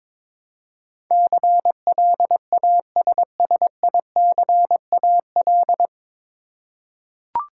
Morse code, heard as a keyed tone, about 22 wpm, 700 Hz. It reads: CLASSICAL
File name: 9 Letter Words - First 3 Letters Given - 22wpm